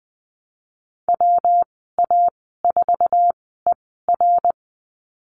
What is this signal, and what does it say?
Morse code, heard as a keyed tone, about 20 wpm, spaced as for 15 wpm, 700 Hz. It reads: WA4ER
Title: 128 Lesson 21g - Number 6 - Call Signs 20wpm